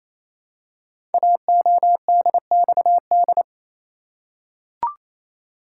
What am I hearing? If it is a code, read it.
Morse code, heard as a keyed tone, about 28 wpm, 700 Hz. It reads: AODXD
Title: Code Groups - 5 Letters - 28wpm